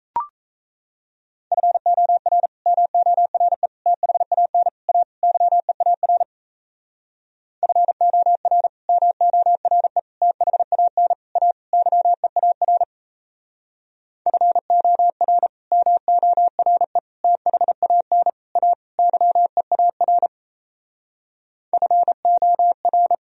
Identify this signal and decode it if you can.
Morse code, heard as a keyed tone, about 37 wpm, 700 Hz. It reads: FOR MORE THAN A YEAR FOR MORE THAN A YEAR FOR MORE THAN A YEAR FOR